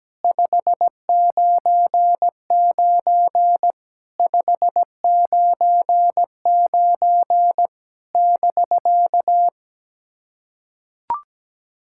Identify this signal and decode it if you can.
Morse code, heard as a keyed tone, about 17 wpm, 700 Hz. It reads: 599 599 <BK>
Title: SOTA QSO Practice 17wpm